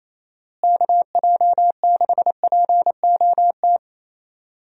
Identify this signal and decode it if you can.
Morse code, heard as a keyed tone, about 28 wpm, 700 Hz. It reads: KJ6POT
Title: USA Call Signs - 6 Characters - Repeated 2x - 28wpm